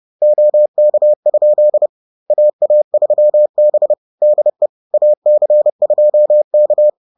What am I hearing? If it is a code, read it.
Morse code, heard as a keyed tone, about 30 wpm, 600 Hz. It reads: OK? AA3B DE AC2K
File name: 30wpm